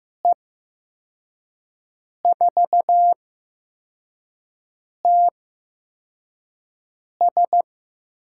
Morse code, heard as a keyed tone, about 15 wpm, spaced as for 4 wpm, 700 Hz. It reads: E4TS
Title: USA Call Signs - 5 Characters - 8x Character Spacing - 15wpm